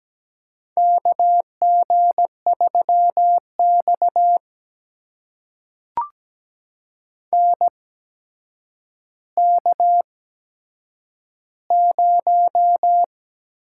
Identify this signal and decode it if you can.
Morse code, heard as a keyed tone, about 17 wpm, spaced as for 8 wpm, 700 Hz. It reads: KG3X NK0
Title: USA Call Signs - 4 Characters - 8x Character Spacing - 17wpm